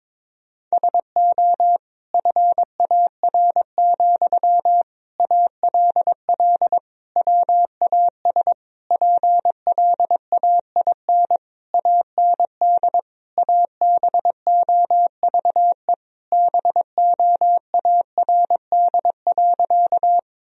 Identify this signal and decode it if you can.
Morse code, heard as a keyed tone, about 22 wpm, 700 Hz. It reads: SO FAR, ALL WAS PLAIN AND ABOVE BOARD.
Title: Sentences from Top 1000 Words - Review Part 4 - 22wpm